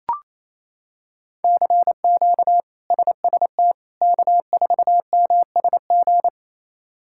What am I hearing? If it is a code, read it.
Morse code, heard as a keyed tone, about 28 wpm, 700 Hz. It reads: CQ SST K4MSG
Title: K1USN SST QSO Practice - 28wpm